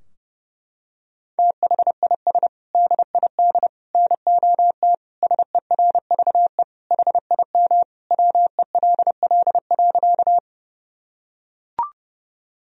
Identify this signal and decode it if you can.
Morse code, heard as a keyed tone, about 31 wpm, 700 Hz. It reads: THIS DID NOT SERVE HIM WELL.
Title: Sentences from Top 1000 Words - Review Part 2 - 30wpm